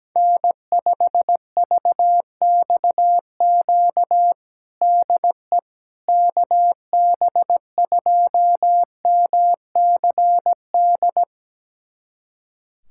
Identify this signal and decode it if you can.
Morse code, heard as a keyed tone, about 17 wpm, 700 Hz. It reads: N5VXQ DE KB2MCD
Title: USA Callsign DE Callsign 17wpm